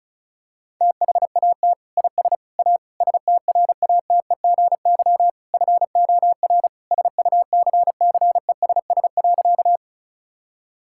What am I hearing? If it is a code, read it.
Morse code, heard as a keyed tone, about 34 wpm, 700 Hz. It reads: THAT IS A STRATEGY FOR SUCCESS.
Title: Sentences from Top 1000 Words - Review Part 5 - 35wpm